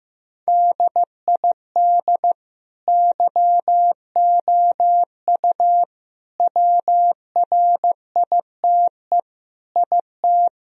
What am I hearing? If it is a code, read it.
Morse code, heard as a keyed tone, about 16 wpm, 700 Hz. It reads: DID YOU WRITE IT
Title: Sentences from Top 1000 Words - Review Part 1 - 15wpm